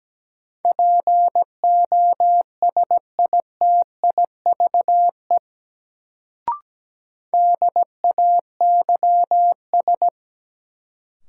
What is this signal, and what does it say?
Morse code, heard as a keyed tone, about 17 wpm, 700 Hz. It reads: POSITIVE DAYS
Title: Sentences from Top 1000 Words - Review Part 5 - New Word Drill - 17wpm